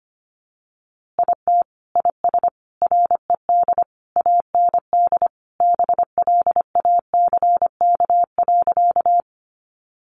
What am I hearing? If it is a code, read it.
Morse code, heard as a keyed tone, about 25 wpm, 700 Hz. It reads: IT IS RED AND BLACK.
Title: Easy Phrases 25wpm